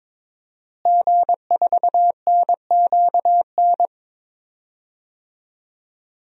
Morse code, heard as a keyed tone, about 22 wpm, 700 Hz. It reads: G4NQN
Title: DXCC-100 Call Signs - 5 Characters - Repeated 3x - 22wpm